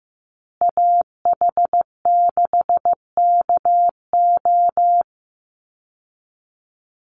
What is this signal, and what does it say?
Morse code, heard as a keyed tone, about 15 wpm, 700 Hz. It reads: AH6KO
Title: Call Signs Repeated 3x - 15wpm